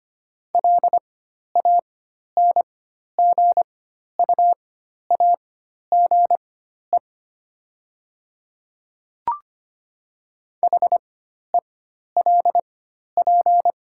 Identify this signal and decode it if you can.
Morse code, heard as a keyed tone, about 25 wpm, spaced as for 12 wpm, 700 Hz. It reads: LANGUAGE HELP
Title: 4x Character Spacing - Top 500 Words - 25wpm